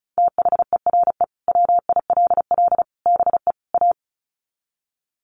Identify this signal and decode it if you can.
Morse code, heard as a keyed tone, about 34 wpm, 700 Hz. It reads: THERE WILL BE A
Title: Sets of 4 Words 35wpm